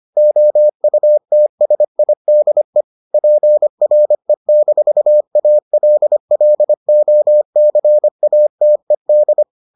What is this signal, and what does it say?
Morse code, heard as a keyed tone, about 25 wpm, 600 Hz. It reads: OUTSIDE PRE-ALLOCATED